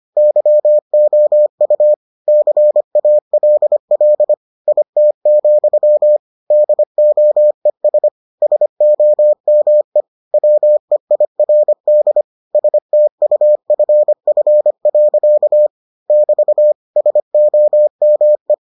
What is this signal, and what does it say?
Morse code, heard as a keyed tone, about 25 wpm, 600 Hz. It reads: YOU CALL IT, DOES SOME WEIRD STUFF. <BT> SOME